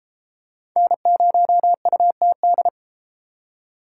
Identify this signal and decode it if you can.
Morse code, heard as a keyed tone, about 33 wpm, 700 Hz. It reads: N0UTD